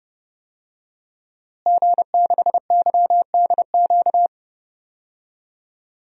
Morse code, heard as a keyed tone, about 30 wpm, 700 Hz. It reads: G6YDQ
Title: DXCC-100 Call Signs - 5 Characters - Repeated 2x - 30wpm